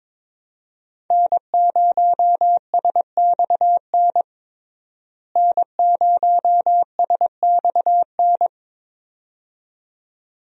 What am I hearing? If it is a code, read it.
Morse code, heard as a keyed tone, about 22 wpm, 700 Hz. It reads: N0SXN N0SXN
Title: USA Call Signs - 5 Characters - Repeated 2x - 22wpm